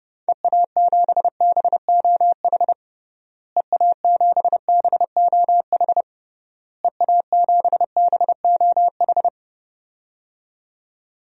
Morse code, heard as a keyed tone, about 30 wpm, 700 Hz. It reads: EA7BOH EA7BOH EA7BOH
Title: DXCC-100 Call Signs - 6 Characters - Repeated 3x - 30wpm